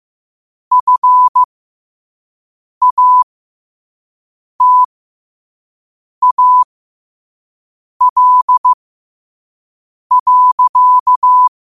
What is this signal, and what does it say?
Morse code, heard as a keyed tone, about 15 wpm, spaced as for 5 wpm, 1000 Hz. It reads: FATAL.
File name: qotdfiles / 5wpm